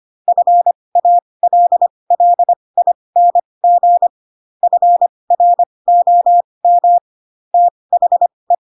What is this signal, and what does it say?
Morse code, heard as a keyed tone, about 25 wpm, spaced as for 20 wpm, 700 Hz. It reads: FALLING FROM THE
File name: Pinocchio / Pinocchio ch7 15 25 ik1hge